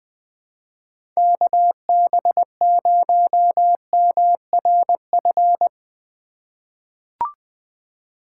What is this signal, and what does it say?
Morse code, heard as a keyed tone, about 20 wpm, 700 Hz. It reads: KB0MRF
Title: USA Call Signs - 6 Characters - 4x Character Spacing - 20wpm